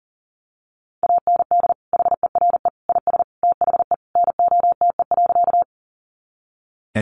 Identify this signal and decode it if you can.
Morse code, heard as a keyed tone, about 40 wpm, 700 Hz. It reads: AND HERE IS THE NOTE.
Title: Sentences from Top 500 Words 40wpm